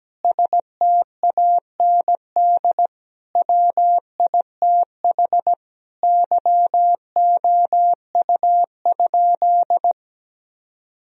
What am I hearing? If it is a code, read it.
Morse code, heard as a keyed tone, about 17 wpm, 700 Hz. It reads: STAND WITH YOU?
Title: Sentences from Top 1000 Words - Review Part 1 - 17wpm